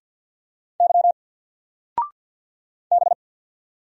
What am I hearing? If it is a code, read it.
Morse code, heard as a keyed tone, about 48 wpm, 700 Hz. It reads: Y B
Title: Single Character (Letter:Number:Punctuation:Prosign) 50wpm